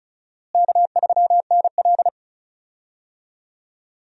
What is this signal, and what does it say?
Morse code, heard as a keyed tone, about 36 wpm, 700 Hz. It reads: K3NL
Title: USA Call Signs - 4 Characters - Repeated 3x - 35wpm